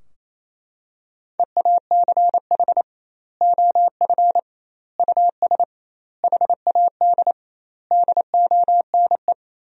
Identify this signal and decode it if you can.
Morse code, heard as a keyed tone, about 28 wpm, 700 Hz. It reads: EACH OF US HAD DONE
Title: 2x Word-Spacing - Sentences from Top 500 Words - 28wpm